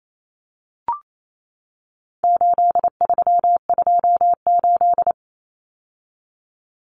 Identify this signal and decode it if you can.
Morse code, heard as a keyed tone, about 28 wpm, 700 Hz. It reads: 8328
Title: Numbers - 4-Digit 28wpm